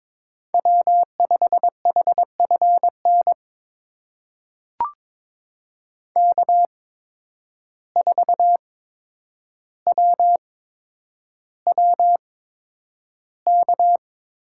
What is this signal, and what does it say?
Morse code, heard as a keyed tone, about 22 wpm, spaced as for 9 wpm, 700 Hz. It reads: W5HFN K4WWK
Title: USA Call Signs - 5 Characters - 8x Character Spacing - 22wpm